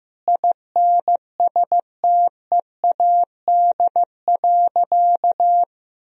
Morse code, heard as a keyed tone, about 15 wpm, 700 Hz. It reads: INSTEAD.
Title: Sentences from Top 1000 Words - Review Part 4 - 15wpm